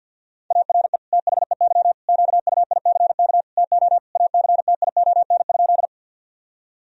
Morse code, heard as a keyed tone, about 51 wpm, 700 Hz. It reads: ARE THEY QUICK TO ACTION?
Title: Sentences from Top 1000 Words - Review Part 4 - 50wpm